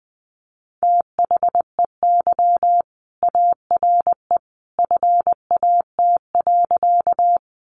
Morse code, heard as a keyed tone, about 20 wpm, 700 Hz. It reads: THEY ARE FAT.
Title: Sentences from Top 600 Words 20wpm